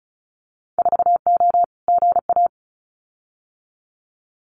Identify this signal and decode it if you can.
Morse code, heard as a keyed tone, about 34 wpm, 700 Hz. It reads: 4O GA
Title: Winter Field Day Exchanges 35wpm